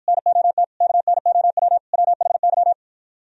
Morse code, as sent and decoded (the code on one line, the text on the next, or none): -. --- - / -.- -. --- .-- / .-- .... -.--
NOT KNOW WHY